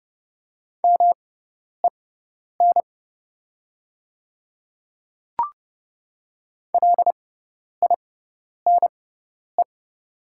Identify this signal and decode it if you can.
Morse code, heard as a keyed tone, about 30 wpm, spaced as for 10 wpm, 700 Hz. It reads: MEN LINE
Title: 6x Character Spacing - Top 200 Words - 30wpm